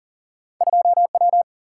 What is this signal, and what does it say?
Morse code, heard as a keyed tone, about 38 wpm, 700 Hz. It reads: 2W
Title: Single-Letter Single-Number 40wpm